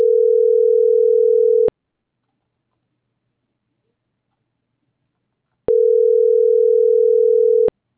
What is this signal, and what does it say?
Ringback tone (North America), 440 + 480 Hz.
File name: CENTRAL TERRESTRE / out karis - .5517